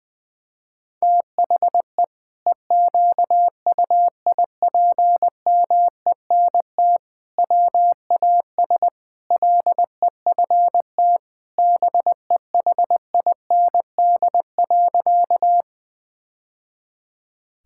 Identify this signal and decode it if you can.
Morse code, heard as a keyed tone, about 21 wpm, 700 Hz. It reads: THE EQUIPMENT WAS LEFT BEHIND.
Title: Sentences from Top 1000 Words - Review Part 5 - 20wpm